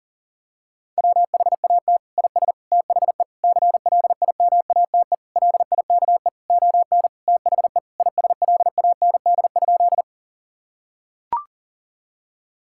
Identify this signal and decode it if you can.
Morse code, heard as a keyed tone, about 40 wpm, 700 Hz. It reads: WHAT IS THE CLIMATE LIKE ON THE ISLAND?
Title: Sentences from Top 1600 Words 40wpm